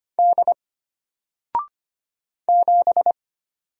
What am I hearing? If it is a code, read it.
Morse code, heard as a keyed tone, about 25 wpm, 700 Hz. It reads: D 7
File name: Single-Letter-Number-25wpm